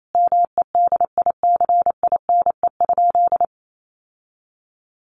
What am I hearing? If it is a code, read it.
Morse code, heard as a keyed tone, about 28 wpm, 700 Hz. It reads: MEDICINE?
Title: Sentences from Top 900 Words 28wpm